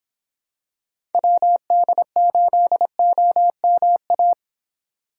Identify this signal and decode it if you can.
Morse code, heard as a keyed tone, about 26 wpm, 700 Hz. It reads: WD8OMA